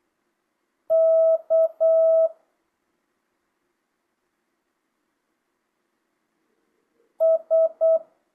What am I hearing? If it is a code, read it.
Morse code, heard as a keyed tone, about 8 wpm, 650 Hz. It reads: K S
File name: jh8cbh / 2025cw